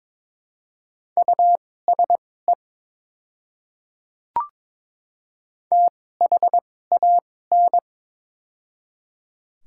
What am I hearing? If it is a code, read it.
Morse code, heard as a keyed tone, about 22 wpm, spaced as for 16 wpm, 700 Hz. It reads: USE THAN
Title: 2x Character Spacing - Top 500 Words - 22wpm